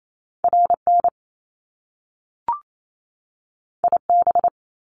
Morse code, heard as a keyed tone, about 28 wpm, 700 Hz. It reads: RN IB
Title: Top Two Letter Combinations in Words 28wpm